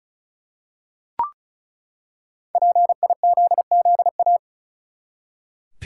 Morse code, heard as a keyed tone, about 36 wpm, 700 Hz. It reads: PIZZA
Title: Five Letter Words - 35wpm